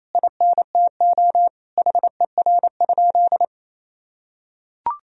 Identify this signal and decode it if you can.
Morse code, heard as a keyed tone, about 28 wpm, 700 Hz. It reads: INTO HER?
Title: Sentences from Top 1000 Words - Review Part 3 - 28wpm